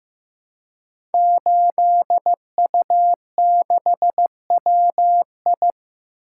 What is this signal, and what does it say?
Morse code, heard as a keyed tone, about 15 wpm, 700 Hz. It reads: 8U6WI
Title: Code Groups - 5 Letter:Numbers - 15wpm